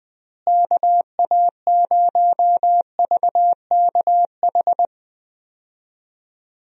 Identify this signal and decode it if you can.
Morse code, heard as a keyed tone, about 20 wpm, 700 Hz. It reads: KA0VKH
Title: USA Call Signs - 6 Characters - Repeated 3x - 20wpm